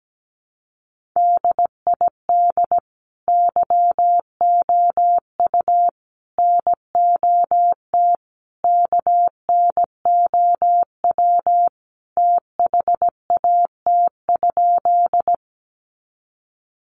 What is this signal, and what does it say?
Morse code, heard as a keyed tone, about 17 wpm, 700 Hz. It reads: DID YOU NOT KNOW THAT?
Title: Sentences from Top 600 Words 17wpm